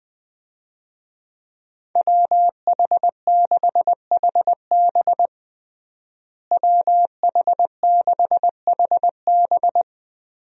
Morse code, heard as a keyed tone, about 20 wpm, 700 Hz. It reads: WH6HB WH6HB
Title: USA Call Signs - 5 Characters - Repeated 3x - 20wpm